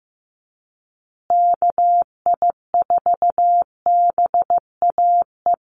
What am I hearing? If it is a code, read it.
Morse code, heard as a keyed tone, about 15 wpm, 700 Hz. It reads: KI4BAE
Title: USA Call Signs Repeated 2x 15wpm